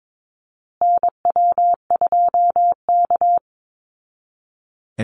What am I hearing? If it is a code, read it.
Morse code, heard as a keyed tone, about 22 wpm, 700 Hz. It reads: NW2K
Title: Call Signs Repeated 2x - 22wpm